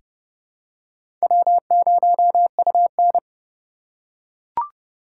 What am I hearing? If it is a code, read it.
Morse code, heard as a keyed tone, about 29 wpm, 700 Hz. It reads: W0UN
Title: USA Call Signs - 4 Characters - 3x Character Spacing - 30wpm